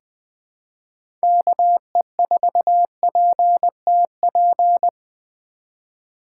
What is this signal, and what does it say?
Morse code, heard as a keyed tone, about 20 wpm, 700 Hz. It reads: KE4PTP
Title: 2023 POTA Hunter Call Signs - 20wpm